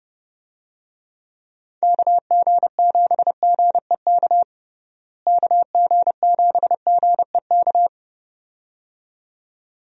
Morse code, heard as a keyed tone, about 30 wpm, 700 Hz. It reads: KG7GEK KG7GEK
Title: USA Call Signs - 6 Characters - Repeated 2x - 30wpm